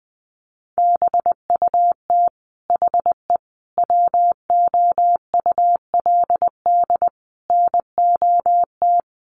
Morse code, heard as a keyed tone, about 20 wpm, 700 Hz. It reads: BUT HE WOULD NOT
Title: Sentences from Top 1100 Words 20wpm